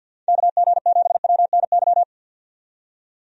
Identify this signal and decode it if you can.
Morse code, heard as a keyed tone, about 53 wpm, 700 Hz. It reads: KG7WNY
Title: USA Call Signs - 6 Characters - 50wpm